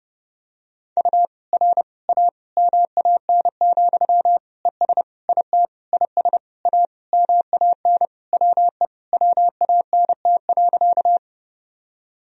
Morse code, heard as a keyed tone, about 32 wpm, 700 Hz. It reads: U R A MAN, ES IT IS A MAN WE WANT.
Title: Sentences from Top 200 Words and Abbreviations 30wpm